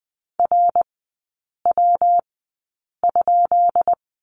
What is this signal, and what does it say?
Morse code, heard as a keyed tone, about 20 wpm, spaced as for 8 wpm, 700 Hz. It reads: RW?
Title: 133 Lesson 22e - Question Mark - Three Characters 20wpm